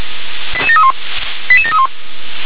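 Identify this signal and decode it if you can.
Five-tone selective call (ZVEI), heard as quick runs of five tones, tones of about 70 ms. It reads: 88731 88731
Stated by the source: ZVEI five-tone call received with an SDR.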